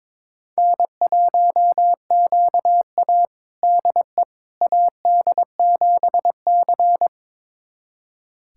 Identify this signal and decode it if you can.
Morse code, heard as a keyed tone, about 22 wpm, 700 Hz. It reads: N1QA DE AD7C
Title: USA Callsign DE Callsign - 4 Characters - 22wpm